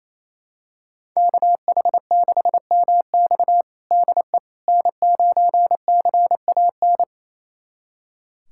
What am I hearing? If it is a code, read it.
Morse code, heard as a keyed tone, about 28 wpm, 700 Hz. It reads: KH6MX DE N9CAN
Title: USA Callsign DE Callsign - 5 Characters - 28wpm